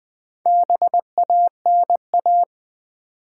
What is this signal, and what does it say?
Morse code, heard as a keyed tone, about 20 wpm, 700 Hz. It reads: BANA
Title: Build A Word - 2 Letters At A Time - 6-Letter Words - 20wpm